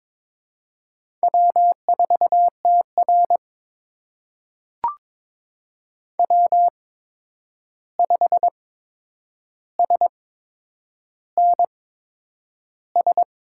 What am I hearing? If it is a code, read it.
Morse code, heard as a keyed tone, about 22 wpm, spaced as for 9 wpm, 700 Hz. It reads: W4TR W5SNS
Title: USA Call Signs - 8x Character Spacing - 22wpm